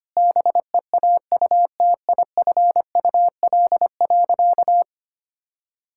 Morse code, heard as a keyed tone, about 25 wpm, 700 Hz. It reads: BEAUTIFUL.
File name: practice / 25wpm / Sentences with 2-Word Phrases - 25wpm